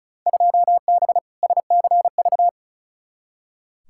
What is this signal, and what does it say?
Morse code, heard as a keyed tone, about 36 wpm, 700 Hz. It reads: 2B SCV
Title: Field Day Exchanges Encore - 35wpm